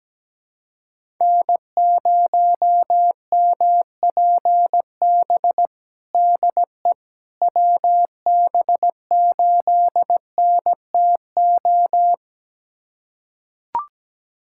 Morse code, heard as a keyed tone, about 17 wpm, 700 Hz. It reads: N0MPB DE WB8NTO
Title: USA Callsign DE Callsign 17wpm